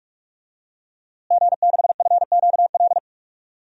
Morse code, heard as a keyed tone, about 48 wpm, 700 Hz. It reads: G6FQL
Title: DXCC-100 Call Signs - 5 Characters - Repeated 2x - 45wpm